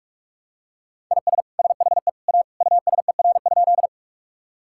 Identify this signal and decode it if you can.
Morse code, heard as a keyed tone, about 41 wpm, 700 Hz. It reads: IS SHE A USER?
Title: Sentences from Top 1000 Words - Review Part 5 - 45wpm